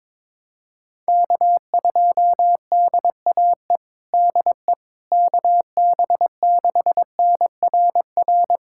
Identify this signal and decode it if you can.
Morse code, heard as a keyed tone, about 22 wpm, 700 Hz. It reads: K2DAE DE KB6NRR
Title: USA Callsign DE Callsign 22wpm